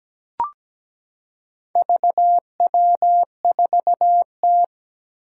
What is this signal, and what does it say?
Morse code, heard as a keyed tone, about 17 wpm, 700 Hz. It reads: VW4T
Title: Four Challenging Letter-Number Combinations 17wpm